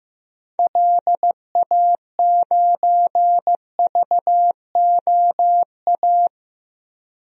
Morse code, heard as a keyed tone, about 15 wpm, 700 Hz. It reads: LA9VOA
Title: DXCC-100 Call Signs - 6 Characters - 15wpm